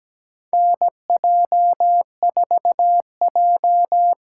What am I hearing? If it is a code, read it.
Morse code, heard as a keyed tone, about 17 wpm, 700 Hz. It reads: NJ4J
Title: USA Callsign DE Callsign - 4 Characters - 17wpm